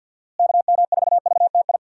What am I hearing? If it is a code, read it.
Morse code, heard as a keyed tone, about 49 wpm, 700 Hz. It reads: KM4VTI
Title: USA Call Signs - 6 Characters - Repeated 2x - 50wpm